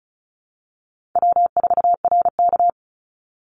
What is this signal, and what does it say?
Morse code, heard as a keyed tone, about 34 wpm, 700 Hz. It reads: W4RK
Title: NA SOTA Chaser Callsigns 35wpm